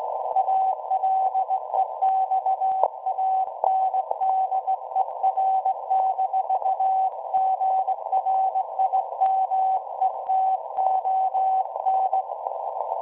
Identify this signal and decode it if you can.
Morse code, heard as a keyed tone, about 18 wpm, 760 Hz. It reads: ALEXANDER-GRIMETON
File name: WebSDR records / SAQ at PI4THT part od 1005z 17-2kHz